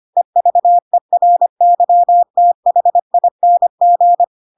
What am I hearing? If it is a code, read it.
Morse code, heard as a keyed tone, about 25 wpm, 700 Hz. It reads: EVERYTHING
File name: Pinocchio / Pinocchio ch7 25 25 ik1hge